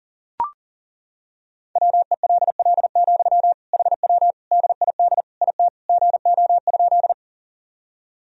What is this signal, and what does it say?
Morse code, heard as a keyed tone, about 40 wpm, 700 Hz. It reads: WELL, HW DID IT GO?
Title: Sentences from Top 200 Words and Abbreviations 40wpm